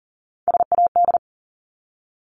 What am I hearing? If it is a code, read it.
Morse code, heard as a keyed tone, about 40 wpm, 700 Hz. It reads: SAD
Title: Top 1501-1750 Words 40wpm